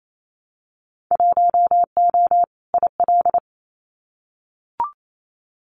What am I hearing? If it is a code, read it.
Morse code, heard as a keyed tone, about 28 wpm, 700 Hz. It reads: 1O IL
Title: Winter Field Day Exchanges 28wpm